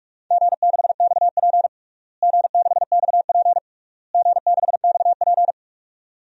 Morse code, heard as a keyed tone, about 46 wpm, 690 Hz. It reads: G6XP G6XP G6XP
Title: DXCC-100 Call Signs - 4 Characters - Repeated 3x - 45wpm